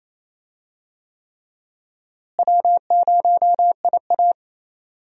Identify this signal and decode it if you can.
Morse code, heard as a keyed tone, about 28 wpm, 700 Hz. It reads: W0IA
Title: USA Call Signs - 4 Characters - Repeated 3x - 28wpm